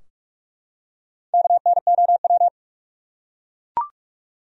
Morse code, heard as a keyed tone, about 43 wpm, 700 Hz. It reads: KNOW
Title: Top 100 Words Repeated 45wpm